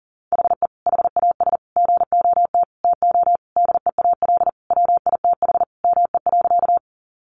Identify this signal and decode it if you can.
Morse code, heard as a keyed tone, about 40 wpm, 700 Hz. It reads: HE HAS GOT TO DEAL WITH ME.
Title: Sentences from Top 800 Words 40wpm